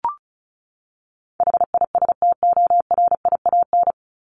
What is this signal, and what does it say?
Morse code, heard as a keyed tone, about 34 wpm, 700 Hz. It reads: HISTORIAN
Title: Top 1751-2000 Words 35wpm